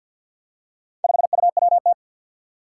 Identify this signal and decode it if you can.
Morse code, heard as a keyed tone, about 53 wpm, 700 Hz. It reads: 5UWT
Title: Four Challenging Letter-Number Combinations 50wpm